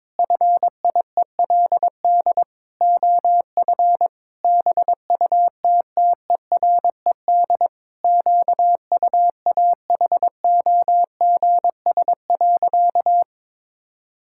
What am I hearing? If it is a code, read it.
Morse code, heard as a keyed tone, about 22 wpm, 700 Hz. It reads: FIELD OF BUTTERED QUAHOGS.
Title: Pangram Challenge v2 - 22wpm